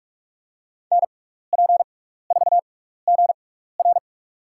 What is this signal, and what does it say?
Morse code, heard as a keyed tone, about 42 wpm, spaced as for 15 wpm, 700 Hz. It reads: NP4GR
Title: USA Call Signs - 5 Characters - 6x Character Spacing - 45wpm